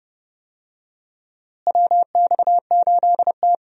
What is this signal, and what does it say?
Morse code, heard as a keyed tone, about 31 wpm, 700 Hz. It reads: WX8T
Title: USA Call Signs - 4 Characters - Repeated 3x - 30wpm